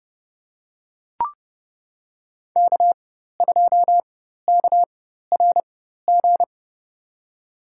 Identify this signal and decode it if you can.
Morse code, heard as a keyed tone, about 31 wpm, spaced as for 14 wpm, 700 Hz. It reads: K2KRG
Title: USA Call Signs - 5 Characters - 4x Character Spacing - 30wpm